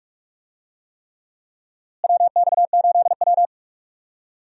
Morse code, heard as a keyed tone, about 46 wpm, 700 Hz. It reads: WX8W
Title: USA Call Signs - 4 Characters - Repeated 3x - 45wpm